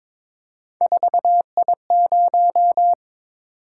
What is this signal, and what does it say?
Morse code, heard as a keyed tone, about 22 wpm, 700 Hz. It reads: 4I0
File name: practice / 22wpm / Three Letter-Number Combinations - 22wpm